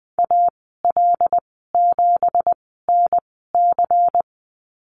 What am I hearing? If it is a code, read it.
Morse code, heard as a keyed tone, about 20 wpm, spaced as for 14 wpm, 700 Hz. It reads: AL7NC
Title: 245 Lesson 39g - Letter z - Call Signs 20wpm